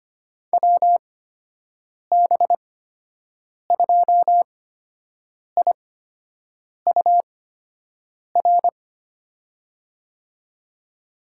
Morse code, heard as a keyed tone, about 25 wpm, spaced as for 7 wpm, 700 Hz. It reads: WB2IUR
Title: USA Call Signs - 8x Character Spacing - 25wpm